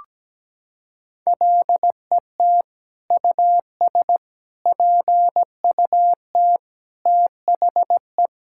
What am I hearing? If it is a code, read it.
Morse code, heard as a keyed tone, about 17 wpm, 700 Hz. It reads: LET US PUT THE
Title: Sentences from Top 1000 Words - Review Part 5 - 17wpm